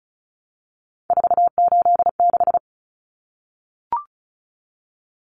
Morse code, heard as a keyed tone, about 34 wpm, 700 Hz. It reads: 486
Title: Common Cut Numbers - 3-Digit 35wpm